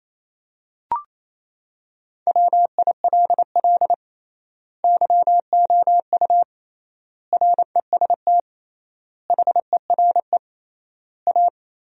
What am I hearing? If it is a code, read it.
Morse code, heard as a keyed tone, about 27 wpm, 700 Hz. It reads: WILL YOU REST HERE A
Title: 3x Word-Spacing - Sentences from Top 500 Words - 28wpm